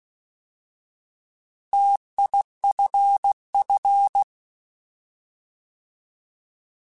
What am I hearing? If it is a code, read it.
Morse code, heard as a keyed tone, about 16 wpm, 800 Hz. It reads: TIFF